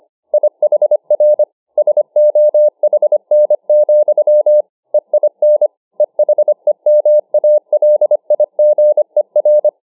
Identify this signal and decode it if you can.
Morse code, heard as a keyed tone, about 25 wpm, 600 Hz. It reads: IHR SOHN, EIN EHEMALIGER